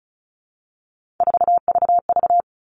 Morse code, heard as a keyed tone, about 34 wpm, 700 Hz. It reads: 4VV
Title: All Cut Numbers - 3-Digit 35wpm